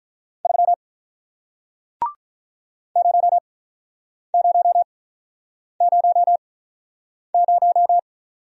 Morse code, heard as a keyed tone, about 48 wpm, spaced as for 9 wpm, 700 Hz. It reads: 3 0000